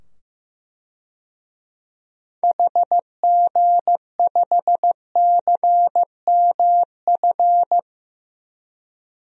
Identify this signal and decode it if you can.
Morse code, heard as a keyed tone, about 15 wpm, 700 Hz. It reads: HG5CMF
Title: DXCC-100 Call Signs - 6 Characters - 15wpm